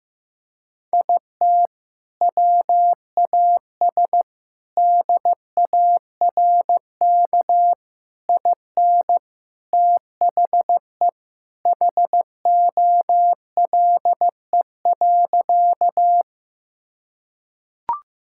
Morse code, heard as a keyed tone, about 15 wpm, 700 Hz. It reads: IT WAS DARK IN THE HOLE.
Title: Sentences from Top 1600 Words 15wpm